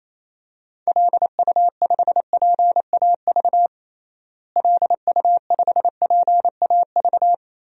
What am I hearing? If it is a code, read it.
Morse code, heard as a keyed tone, about 28 wpm, 700 Hz. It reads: LU5PAV LU5PAV
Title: DXCC-100 Call Signs - 6 Characters - Repeated 2x - 28wpm